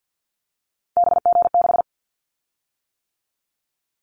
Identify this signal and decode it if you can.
Morse code, heard as a keyed tone, about 50 wpm, 700 Hz. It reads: BG6
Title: Callsign Prefixes for Top 100 DXCC - 50wpm